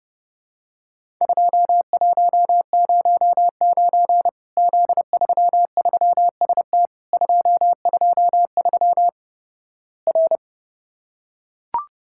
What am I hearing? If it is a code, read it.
Morse code, heard as a keyed tone, about 30 wpm, 700 Hz. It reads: 2109 Z33ST 223 R
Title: Worked All Europe DX Contest - QTC Exchange - 30wpm